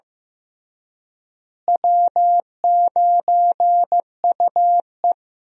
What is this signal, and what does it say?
Morse code, heard as a keyed tone, about 15 wpm, 700 Hz. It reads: W9UE
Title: USA Call Signs - 4 Characters - Repeated 3x - 15wpm